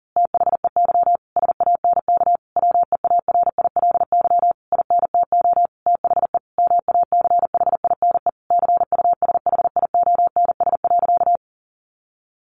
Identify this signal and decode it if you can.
Morse code, heard as a keyed tone, about 40 wpm, 700 Hz. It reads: THEY SANK WEARILY INTO THE MACHINE CUSHIONS.
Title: A Sound Of Thunder 40wpm